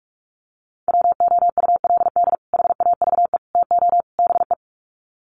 Morse code, heard as a keyed tone, about 44 wpm, 700 Hz. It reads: WOULD HAVE TO BE
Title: Sets of 4 Words 45wpm